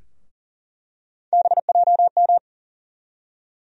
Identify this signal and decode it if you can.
Morse code, heard as a keyed tone, about 40 wpm, 700 Hz. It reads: BJM
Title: USA Call Sign Postfixes - 3 Letters 40wpm